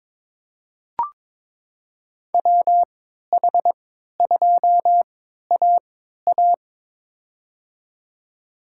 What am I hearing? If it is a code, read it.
Morse code, heard as a keyed tone, about 23 wpm, spaced as for 13 wpm, 700 Hz. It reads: WH2AA
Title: USA Call Signs - 5 Characters - 3x Character Spacing - 22wpm